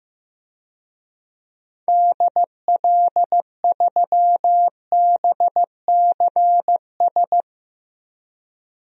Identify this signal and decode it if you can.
Morse code, heard as a keyed tone, about 15 wpm, 700 Hz. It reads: DL3BCS
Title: DXCC-100 Call Signs - 6 Characters - 15wpm